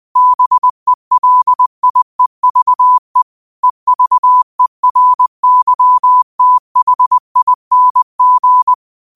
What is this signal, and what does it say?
Morse code, heard as a keyed tone, about 20 wpm, 1000 Hz. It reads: BELIEVE EVERYTHING
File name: qotdfiles / 20wpm